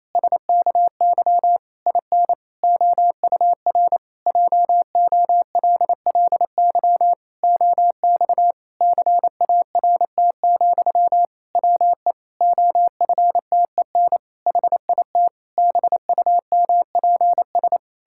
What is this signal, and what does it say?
Morse code, heard as a keyed tone, about 28 wpm, 700 Hz. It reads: SKY IN OUR JOLLY OX CART, WE OFTEN HIT BUMPS
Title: Pangram Challenge v2 - 28wpm